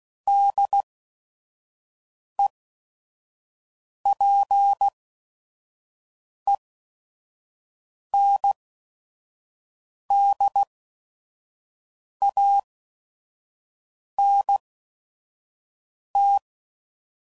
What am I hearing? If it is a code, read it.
Morse code, heard as a keyed tone, about 16 wpm, spaced as for 5 wpm, 800 Hz. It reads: DEPENDANT
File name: CW Code Groups @5WPM / Technical Subjects / XL